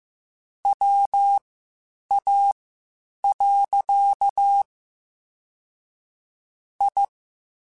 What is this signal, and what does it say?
Morse code, heard as a keyed tone, about 15 wpm, spaced as for 8 wpm, 800 Hz. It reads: WA. I
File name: (spaced) / QSOs Spaced-mp3s / WA7QOX 2 @13 sp